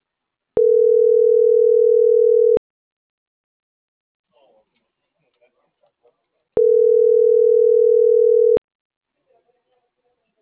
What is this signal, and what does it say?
Ringback tone (North America), 440 + 480 Hz.